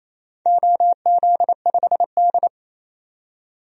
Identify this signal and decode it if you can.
Morse code, heard as a keyed tone, about 28 wpm, 700 Hz. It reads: OZ5D
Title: DXCC-100 Call Signs - 4 Characters - 28wpm